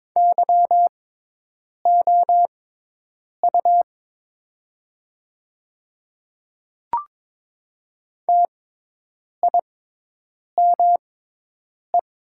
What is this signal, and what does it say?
Morse code, heard as a keyed tone, about 22 wpm, spaced as for 8 wpm, 700 Hz. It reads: YOU TIME
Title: 6x Character Spacing - Top 100 Words - 22wpm